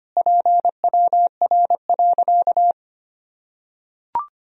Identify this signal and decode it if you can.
Morse code, heard as a keyed tone, about 25 wpm, 700 Hz. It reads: PWR.
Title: Sentences from Top 500 Words and Abbreviations 25wpm